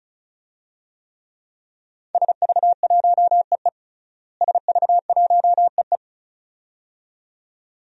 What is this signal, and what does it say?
Morse code, heard as a keyed tone, about 34 wpm, 700 Hz. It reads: SV1EE SV1EE
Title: DXCC-100 Call Signs - 5 Characters - Repeated 2x - 35wpm